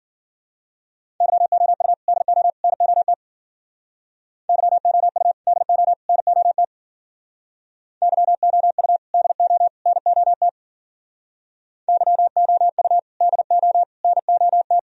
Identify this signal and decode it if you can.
Morse code, heard as a keyed tone, about 56 wpm, 700 Hz. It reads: YOU DO NOT YOU DO NOT YOU DO NOT YOU DO NOT